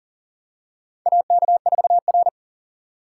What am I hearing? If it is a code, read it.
Morse code, heard as a keyed tone, about 37 wpm, 700 Hz. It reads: AK4R
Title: USA Call Signs - 4 Characters - 4x Character Spacing - 40wpm